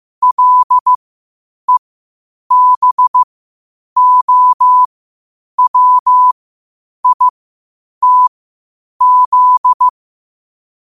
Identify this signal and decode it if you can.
Morse code, heard as a keyed tone, about 15 wpm, spaced as for 8 wpm, 1000 Hz. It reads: LEBOWITZ
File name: 8wpm